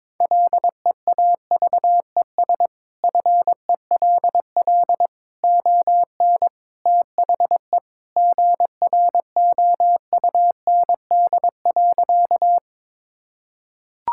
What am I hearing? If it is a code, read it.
Morse code, heard as a keyed tone, about 22 wpm, 700 Hz. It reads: LEAVES FELL ON THE GROUND.
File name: practice / 22wpm / Sentences with 3-Word Phrases - 22wpm